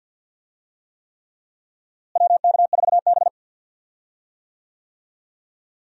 Morse code, heard as a keyed tone, about 51 wpm, 700 Hz. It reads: WK4B
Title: USA Call Signs - 4 Characters - Repeated 3x - 50wpm